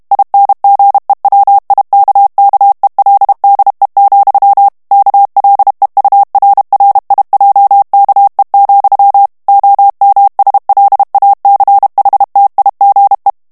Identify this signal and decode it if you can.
Morse code, heard as a keyed tone, about 32 wpm, 800 Hz. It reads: INGEWIKKELDE, KLEURRIJKE, OMSLACHTIGE
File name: HOLLANDAIS texte clair 2 vitesse 30 mots minute